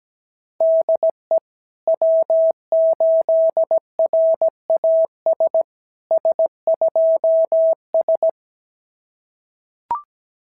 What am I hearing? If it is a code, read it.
Morse code, heard as a keyed tone, about 17 wpm, 650 Hz. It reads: DE W8RAS S2S
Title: SOTA QSO Practice 17wpm